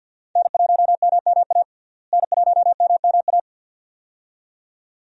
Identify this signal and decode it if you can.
Morse code, heard as a keyed tone, about 50 wpm, 700 Hz. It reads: N1MMA N1MMA
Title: USA Call Signs - 5 Characters - Repeated 2x - 50wpm